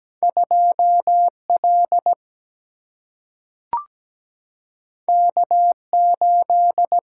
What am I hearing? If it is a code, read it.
Morse code, heard as a keyed tone, about 17 wpm, 700 Hz. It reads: 2L K8
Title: Single-Letter Single-Number 17wpm